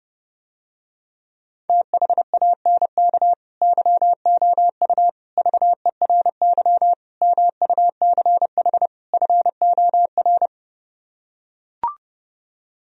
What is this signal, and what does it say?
Morse code, heard as a keyed tone, about 31 wpm, 700 Hz. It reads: THANK YOU VERY MUCH FOR